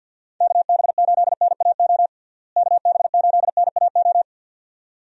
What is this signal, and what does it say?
Morse code, heard as a keyed tone, about 52 wpm, 700 Hz. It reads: KB8NAO KB8NAO
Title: USA Call Signs - 6 Characters - Repeated 2x - 50wpm